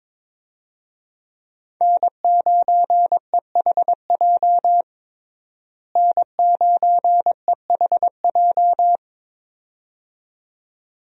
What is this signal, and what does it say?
Morse code, heard as a keyed tone, about 22 wpm, 700 Hz. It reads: N9EHJ N9EHJ
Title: USA Call Signs - 5 Characters - Repeated 2x - 22wpm